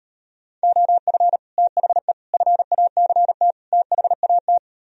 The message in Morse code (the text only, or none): OF THE FACT THAT